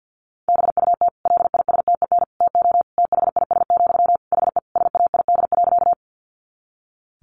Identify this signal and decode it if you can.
Morse code, heard as a keyed tone, about 50 wpm, 700 Hz. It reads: BUT LISTEN TO THIS, HE SAID.
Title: Sentences from Top 400 Words 50wpm